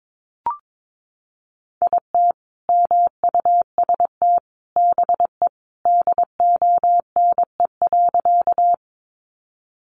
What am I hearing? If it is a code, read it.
Morse code, heard as a keyed tone, about 22 wpm, 700 Hz. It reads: IT MUST BE DONE.
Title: Sentences from Top 500 Words 22wpm